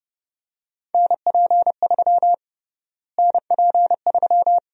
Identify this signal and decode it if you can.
Morse code, heard as a keyed tone, about 32 wpm, 700 Hz. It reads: NP3 NP3
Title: USA Call Sign Prefixes - Repeated 2x - 30wpm